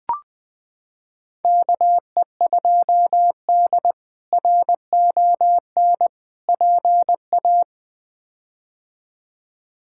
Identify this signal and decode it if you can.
Morse code, heard as a keyed tone, about 20 wpm, 700 Hz. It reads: KE2D RON PA
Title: K1USN SST QSO Practice - 20wpm